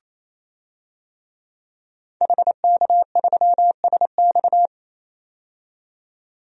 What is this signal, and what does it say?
Morse code, heard as a keyed tone, about 28 wpm, 700 Hz. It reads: HK3SX
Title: DXCC-100 Call Signs - 5 Characters - 28wpm